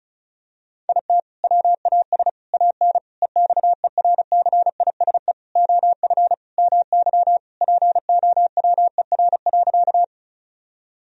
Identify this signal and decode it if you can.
Morse code, heard as a keyed tone, about 34 wpm, 700 Hz. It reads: IT WAS AN EXERCISE OF MY POWER.
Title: Sentences from Top 1000 Words - Review Part 4 - 35wpm